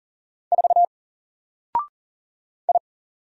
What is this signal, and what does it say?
Morse code, heard as a keyed tone, about 40 wpm, 700 Hz. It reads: 4 I
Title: Single Character (Letter:Number:Punctuation:Prosign) Alt 40wpm